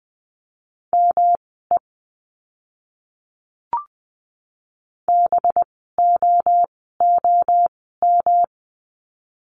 Morse code, heard as a keyed tone, about 20 wpm, spaced as for 14 wpm, 700 Hz. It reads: ME BOOM
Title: 185 Lesson 30f - Letter b - Words 20wpm